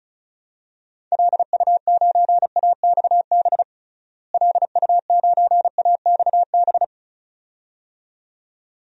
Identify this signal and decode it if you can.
Morse code, heard as a keyed tone, about 36 wpm, 700 Hz. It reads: LU9AXB LU9AXB
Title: DXCC-100 Call Signs - 6 Characters - Repeated 2x - 35wpm